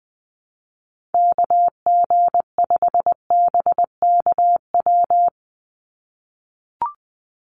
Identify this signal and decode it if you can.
Morse code, heard as a keyed tone, about 20 wpm, 700 Hz. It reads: KG5BKW
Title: USA Call Signs 20wpm